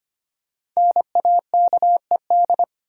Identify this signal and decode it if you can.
Morse code, heard as a keyed tone, about 25 wpm, 700 Hz. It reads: NAKED
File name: 5 Letter Words - First 3 Letters Given - 25wpm